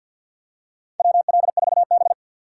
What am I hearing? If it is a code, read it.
Morse code, heard as a keyed tone, about 52 wpm, 700 Hz. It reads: WL4B
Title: USA Call Signs - 4 Characters - 3x Character Spacing - 50wpm